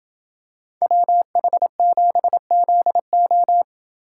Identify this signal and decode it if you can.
Morse code, heard as a keyed tone, about 27 wpm, 700 Hz. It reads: WH7ZO